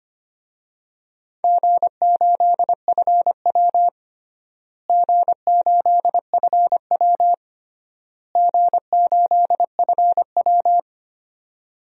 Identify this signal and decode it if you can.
Morse code, heard as a keyed tone, about 25 wpm, 700 Hz. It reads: G8FW G8FW G8FW
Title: DXCC-100 Call Signs - 4 Characters - Repeated 3x - 25wpm